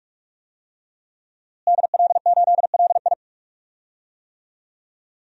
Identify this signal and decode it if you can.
Morse code, heard as a keyed tone, about 46 wpm, 700 Hz. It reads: DL8LI
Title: DXCC-100 Call Signs - 5 Characters - Repeated 3x - 45wpm